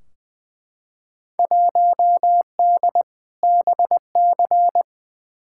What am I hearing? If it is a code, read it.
Morse code, heard as a keyed tone, about 21 wpm, 700 Hz. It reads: 1D BC
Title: Field Day Exchanges - 20wpm